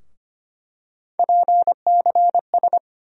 Morse code, heard as a keyed tone, about 25 wpm, 700 Hz. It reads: PCS
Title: USA Call Sign Postfixes - 3 Letters 25wpm